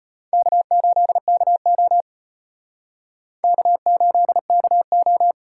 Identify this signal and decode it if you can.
Morse code, heard as a keyed tone, about 32 wpm, 700 Hz. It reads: K8KO K8KO